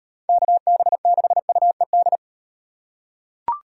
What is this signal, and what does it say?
Morse code, heard as a keyed tone, about 36 wpm, 700 Hz. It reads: KB6UED